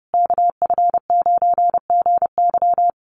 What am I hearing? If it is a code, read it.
Morse code, heard as a keyed tone, about 30 wpm, 700 Hz. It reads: KF9GY
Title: USA Call Signs Repeated 2x 30wpm